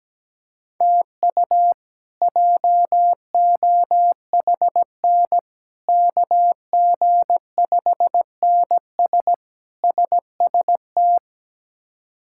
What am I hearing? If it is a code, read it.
Morse code, heard as a keyed tone, about 17 wpm, 700 Hz. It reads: TU JOHN KG5NS SST
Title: K1USN SST QSO Practice - 17wpm